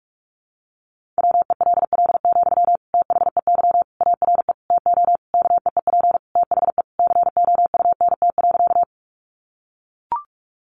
Morse code, heard as a keyed tone, about 44 wpm, 700 Hz. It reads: WELL, THEY ARE TO KEEP THE COUNT.
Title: Sentences from Top 1500 Words 45wpm